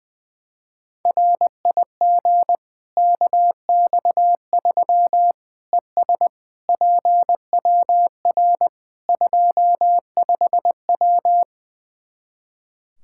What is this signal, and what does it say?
Morse code, heard as a keyed tone, about 20 wpm, 700 Hz. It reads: RIG KX3 ES PWR 25W
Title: K1USN SST QSO Practice 20wpm